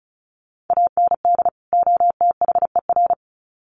Morse code, heard as a keyed tone, about 34 wpm, 700 Hz. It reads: AND OTHER
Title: Sets of 2 Words 35wpm